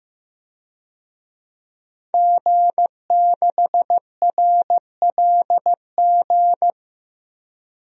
Morse code, heard as a keyed tone, about 15 wpm, 700 Hz. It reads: G6RLG
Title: DXCC-100 Call Signs - 5 Characters - Repeated 3x - 15wpm